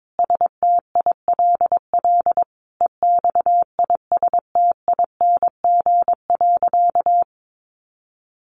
Morse code, heard as a keyed tone, about 22 wpm, 700 Hz. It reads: STILL EXISTING.
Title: Sentences from Top 1500 Words 22wpm